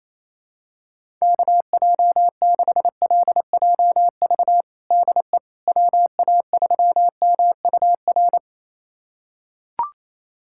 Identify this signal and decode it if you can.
Morse code, heard as a keyed tone, about 28 wpm, 700 Hz. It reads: KJ6LJV DE WA3MUR
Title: USA Callsign DE Callsign - 6 Characters - 28wpm